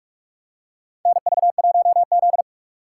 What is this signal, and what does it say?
Morse code, heard as a keyed tone, about 43 wpm, 690 Hz. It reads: NV1Z
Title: USA Call Signs - 4 Characters - 3x Character Spacing - 45wpm